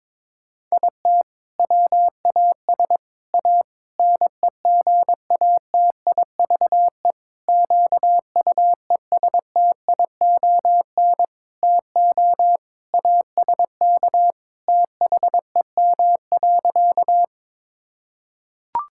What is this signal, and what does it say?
Morse code, heard as a keyed tone, about 22 wpm, 700 Hz. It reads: IT WAS A NEGATIVE QUESTION TO ASK THEM.
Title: Sentences from Top 1000 Words - Review Part 6 - 22wpm